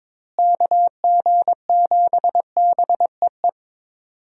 Morse code, heard as a keyed tone, about 22 wpm, 700 Hz. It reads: KG7BEE
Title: USA Call Signs - 6 Characters - Repeated 3x - 22wpm